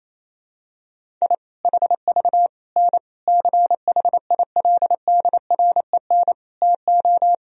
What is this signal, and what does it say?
Morse code, heard as a keyed tone, about 28 wpm, 700 Hz. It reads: I HV N CHILDREN TO
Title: Sentences from Top 400 Words and Abbreviations 28wpm